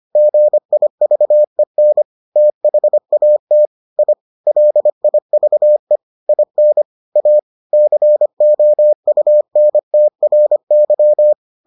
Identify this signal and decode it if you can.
Morse code, heard as a keyed tone, about 25 wpm, 600 Hz. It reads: GIVEN THAT I LIVE IN A COUNTRY